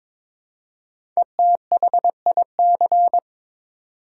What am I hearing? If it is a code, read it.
Morse code, heard as a keyed tone, about 22 wpm, 700 Hz. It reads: ETHIC
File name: 5 Letter Words - First 3 Letters Given - 22wpm